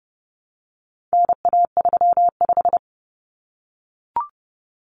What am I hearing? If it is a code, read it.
Morse code, heard as a keyed tone, about 30 wpm, 700 Hz. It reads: NA35
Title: Common Cut Numbers - 4-Digit 30wpm